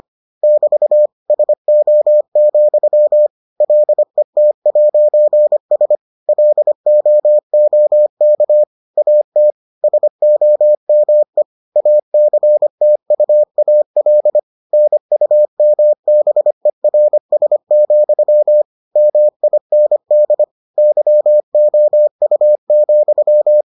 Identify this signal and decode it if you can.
Morse code, heard as a keyed tone, about 25 wpm, 600 Hz. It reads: <BT> SO, LET'S LOOK AT SOME ACTUAL NUMBERS, MIND YOU,